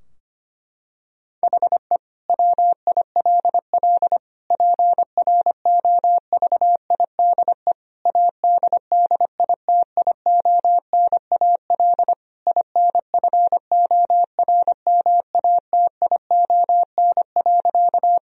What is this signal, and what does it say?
Morse code, heard as a keyed tone, about 25 wpm, 700 Hz. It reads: HE WILL PROVIDE ADDITIONAL INFORMATION.
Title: Sentences from Top 1000 Words - Review Part 4 - 25wpm